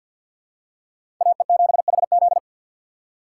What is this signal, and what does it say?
Morse code, heard as a keyed tone, about 52 wpm, 700 Hz. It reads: AE7HZ
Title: USA Call Signs - 5 Characters - 2x Character Spacing - 50wpm